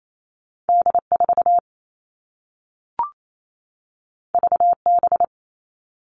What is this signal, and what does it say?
Morse code, heard as a keyed tone, about 28 wpm, 700 Hz. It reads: D4 VB
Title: All Cut Numbers - 2-Digit 28wpm